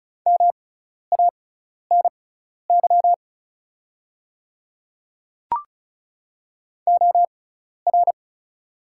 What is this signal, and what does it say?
Morse code, heard as a keyed tone, about 36 wpm, spaced as for 12 wpm, 700 Hz. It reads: MANY OR
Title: 6x Character Spacing - Top 100 Words - 35wpm